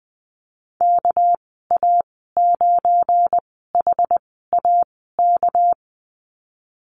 Morse code, heard as a keyed tone, about 20 wpm, spaced as for 14 wpm, 700 Hz. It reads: KA9HAK
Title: 238 Lesson 38g - Letter q - Call Signs 20wpm